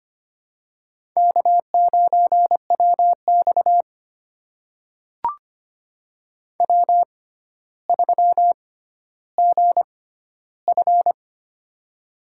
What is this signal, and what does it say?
Morse code, heard as a keyed tone, about 25 wpm, spaced as for 13 wpm, 700 Hz. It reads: K9WX W3GF
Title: USA Call Signs - 4 Characters - 6x Character Spacing - 25wpm